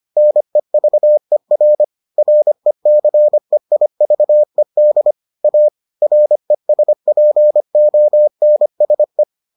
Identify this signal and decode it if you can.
Morse code, heard as a keyed tone, about 25 wpm, 600 Hz. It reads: NEVER RECEIVED A RESPONSE